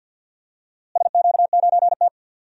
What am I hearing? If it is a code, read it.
Morse code, heard as a keyed tone, about 52 wpm, 700 Hz. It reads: SQ9T
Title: DXCC-100 Call Signs - 4 Characters - Repeated 3x - 50wpm